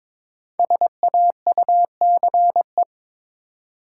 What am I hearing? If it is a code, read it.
Morse code, heard as a keyed tone, about 22 wpm, 700 Hz. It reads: SAUCE
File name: 5 Letter Words - First 3 Letters Given - 22wpm